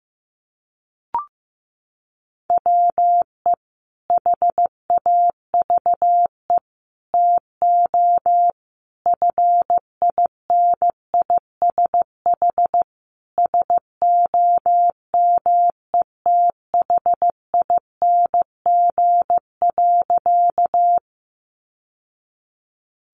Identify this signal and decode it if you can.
Morse code, heard as a keyed tone, about 15 wpm, 700 Hz. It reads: WE HAVE TO FINISH SOMETHING.
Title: Sentences from Top 1200 Words 15wpm